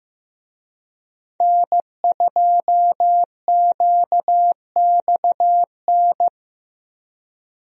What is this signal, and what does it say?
Morse code, heard as a keyed tone, about 15 wpm, 700 Hz. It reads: N2QXN
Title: USA Call Signs - 5 Characters - 15wpm